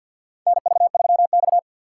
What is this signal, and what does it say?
Morse code, heard as a keyed tone, about 53 wpm, 700 Hz. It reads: NV3X
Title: USA Call Signs - 4 Characters - 2x Character Spacing - 50wpm